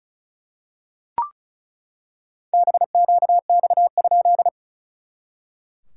Morse code, heard as a keyed tone, about 34 wpm, 700 Hz. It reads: BQX?
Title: SOTA QSO Practice 35wpm